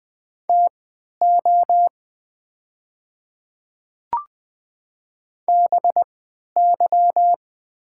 Morse code, heard as a keyed tone, about 20 wpm, spaced as for 11 wpm, 700 Hz. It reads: TO BY
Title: 3x Character Spacing - Top 100 Words - 20wpm